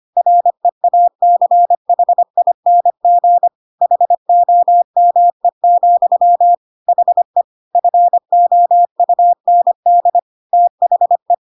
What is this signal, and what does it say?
Morse code, heard as a keyed tone, about 25 wpm, 700 Hz. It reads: REACHING HOME, HE FOUND THE
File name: Pinocchio / Pinocchio ch4 25 25 ik1hge